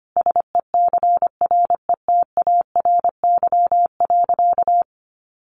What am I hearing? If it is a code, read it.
Morse code, heard as a keyed tone, about 25 wpm, 700 Hz. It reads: SECRETARY.
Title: Sentences from Top 1300 Words 25wpm